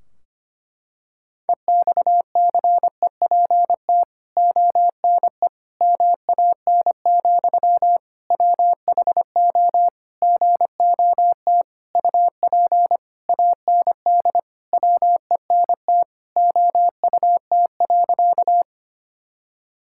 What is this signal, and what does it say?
Morse code, heard as a keyed tone, about 25 wpm, 700 Hz. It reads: EXCEPT ONE MAN, WHO GOT UP AND WENT OUT.
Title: Sentences from Top 1600 Words 25wpm